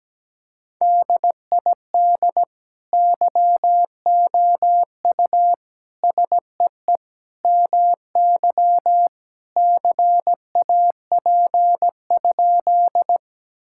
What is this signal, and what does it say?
Morse code, heard as a keyed tone, about 17 wpm, 700 Hz. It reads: DID YOU SEE MY CAP?
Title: Sentences from Top 1600 Words 17wpm